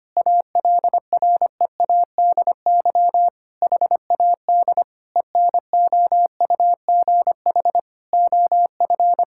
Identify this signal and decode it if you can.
Morse code, heard as a keyed tone, about 25 wpm, 700 Hz. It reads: ALREADY HAD ENOUGH OF
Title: Sentences from Top 1600 Words 25wpm